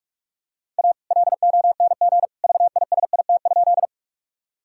Morse code, heard as a keyed tone, about 46 wpm, 700 Hz. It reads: A LONG VISIT?
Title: Sentences from Top 1000 Words - Review Part 6 - 45wpm